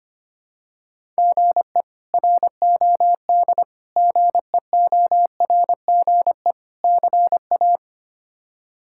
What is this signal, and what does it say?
Morse code, heard as a keyed tone, about 25 wpm, 700 Hz. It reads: GE ROD GEORGE CA
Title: K1USN SST QSO Practice - 25wpm